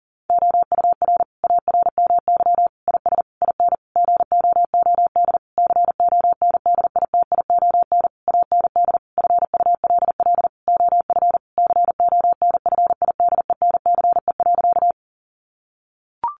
Morse code, heard as a keyed tone, about 40 wpm, 700 Hz. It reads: OUR ARMY IS IN GOOD CONDITION AND FULL OF CONFIDENCE.
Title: Sentences from Top 1500 Words 40wpm